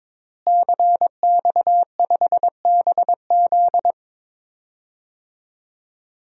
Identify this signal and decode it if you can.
Morse code, heard as a keyed tone, about 22 wpm, 700 Hz. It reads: CX5BZ
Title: DXCC-100 Call Signs - 5 Characters - Repeated 2x - 22wpm